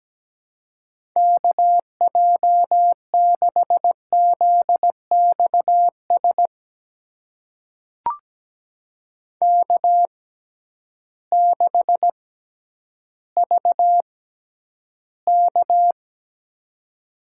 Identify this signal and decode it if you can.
Morse code, heard as a keyed tone, about 17 wpm, spaced as for 10 wpm, 700 Hz. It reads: KJ6ZXS K6VK
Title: USA Call Signs - 6x Character Spacing - 17wpm